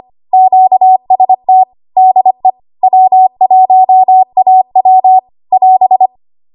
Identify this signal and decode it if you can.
Morse code, heard as a keyed tone, about 25 wpm, 750 Hz. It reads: QST DE W1AW <AS>
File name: Archive / 25 WPM / 25WPM